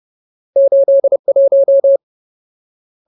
Morse code, heard as a keyed tone, about 30 wpm, 550 Hz. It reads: 81
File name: Ultimate Word List / 30WPM Ultimate